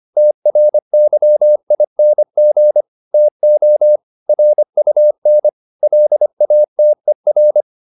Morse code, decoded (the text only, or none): TRYING TO RUN LATER